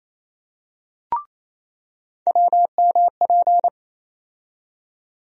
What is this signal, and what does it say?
Morse code, heard as a keyed tone, about 27 wpm, 700 Hz. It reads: WMP
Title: USA Call Sign Postfixes - 3 Letters 28wpm